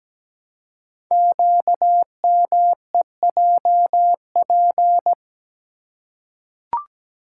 Morse code, heard as a keyed tone, about 17 wpm, 700 Hz. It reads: QMEJP
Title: Code Groups - 5 Letter:Numbers - 17wpm